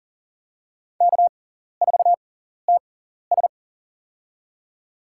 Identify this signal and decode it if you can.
Morse code, heard as a keyed tone, about 40 wpm, spaced as for 14 wpm, 700 Hz. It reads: K4TS
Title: USA Call Signs - 4 Characters - 6x Character Spacing - 40wpm